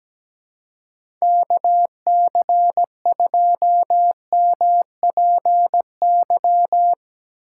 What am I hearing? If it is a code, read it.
Morse code, heard as a keyed tone, about 17 wpm, 700 Hz. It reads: KC2MPY
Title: USA Call Signs - 6 Characters - 2x Character Spacing - 17wpm